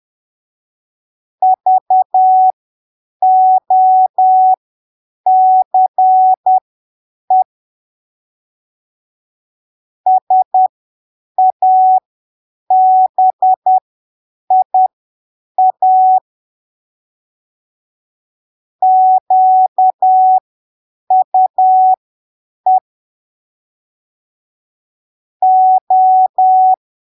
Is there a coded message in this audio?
Yes: Morse code from 1.4 to 26.7 s, heard as a keyed tone, about 10 wpm, spaced as for 7 wpm, 750 Hz. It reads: VOCE SABIA QUE O